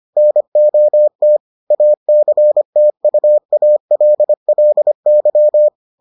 Morse code, heard as a keyed tone, about 25 wpm, 600 Hz. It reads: NOT ACTUALLY